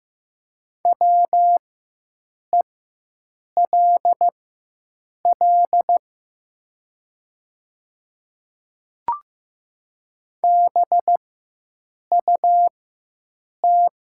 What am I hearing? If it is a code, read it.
Morse code, heard as a keyed tone, about 15 wpm, spaced as for 7 wpm, 700 Hz. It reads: WELL BUT
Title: 4x Character Spacing - Top 200 Words - 15wpm